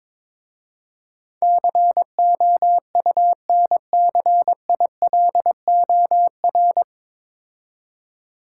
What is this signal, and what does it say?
Morse code, heard as a keyed tone, about 21 wpm, 700 Hz. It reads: COUNCILOR
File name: 9 Letter Words - First 3 Letters Given - 22wpm